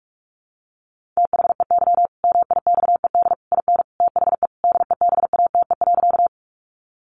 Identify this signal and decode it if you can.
Morse code, heard as a keyed tone, about 44 wpm, 700 Hz. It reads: THEY MIXED IN THE DEBATE.
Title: Sentences from Top 1500 Words 45wpm